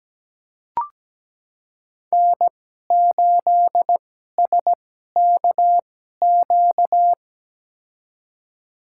Morse code, heard as a keyed tone, about 17 wpm, spaced as for 12 wpm, 700 Hz. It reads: N8SKQ
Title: USA Call Signs - 5 Characters - 2x Character Spacing - 17wpm